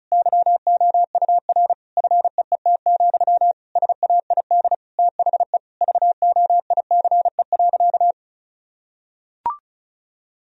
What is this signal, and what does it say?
Morse code, heard as a keyed tone, about 34 wpm, 690 Hz. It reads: YOUR FEET, SAID THE VOICE.
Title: Sentences from Top 1000 Words - Review Part 1 - 35wpm